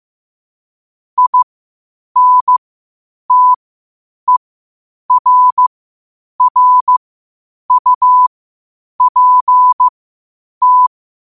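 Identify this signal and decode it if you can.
Morse code, heard as a keyed tone, about 15 wpm, spaced as for 8 wpm, 1000 Hz. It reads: INTERRUPT
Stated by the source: Morse code quotes at 8 WPM.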